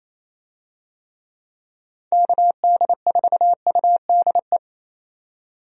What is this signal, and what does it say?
Morse code, heard as a keyed tone, about 28 wpm, 700 Hz. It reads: KD4UDE
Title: USA Call Signs - 6 Characters - 28wpm